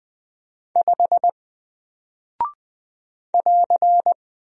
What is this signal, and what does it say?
Morse code, heard as a keyed tone, about 20 wpm, 700 Hz. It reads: 5 <AR>
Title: Single Character (Letter:Number:Punctuation:Prosign) Alt 20wpm